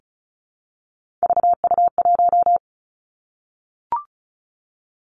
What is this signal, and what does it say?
Morse code, heard as a keyed tone, about 36 wpm, 700 Hz. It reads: VU1
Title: Three Challenging Letter-Number Combinations 35wpm